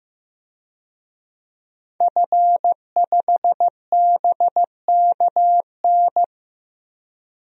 Morse code, heard as a keyed tone, about 15 wpm, 700 Hz. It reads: F5BKN
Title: DXCC-100 Call Signs - 5 Characters - 15wpm